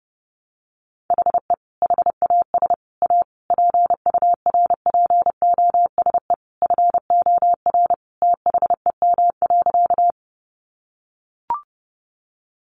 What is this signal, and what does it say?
Morse code, heard as a keyed tone, about 30 wpm, 700 Hz. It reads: HE HAS A PURPOSE FOR THEM.
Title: Sentences from Top 800 Words 30wpm